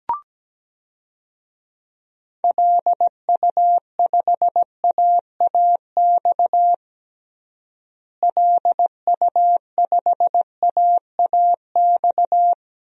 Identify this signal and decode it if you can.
Morse code, heard as a keyed tone, about 17 wpm, 700 Hz. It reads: LU5AAX LU5AAX
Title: DXCC-100 Call Signs - 6 Characters - Repeated 3x - 17wpm